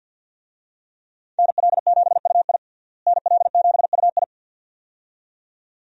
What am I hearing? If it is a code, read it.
Morse code, heard as a keyed tone, about 46 wpm, 700 Hz. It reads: NL7UI NL7UI
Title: USA Call Signs - 5 Characters - Repeated 2x - 50wpm